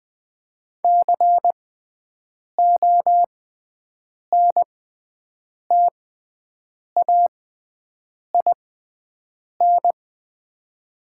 Morse code, heard as a keyed tone, about 20 wpm, spaced as for 7 wpm, 700 Hz. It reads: CONTAIN
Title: 6x Character Spacing - Top 500 Words - 20wpm